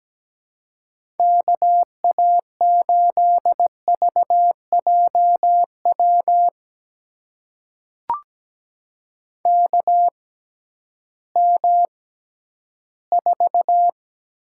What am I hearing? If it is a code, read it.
Morse code, heard as a keyed tone, about 17 wpm, spaced as for 11 wpm, 700 Hz. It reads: KA8VJW KM4
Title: USA Call Signs - 6 Characters - 6x Character Spacing - 17wpm